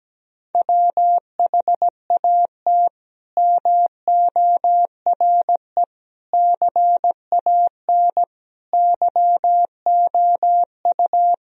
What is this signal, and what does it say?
Morse code, heard as a keyed tone, about 17 wpm, 700 Hz. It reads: WHAT MORE CAN YOU
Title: Sentences from Top 1000 Words - Review Part 4 - 17wpm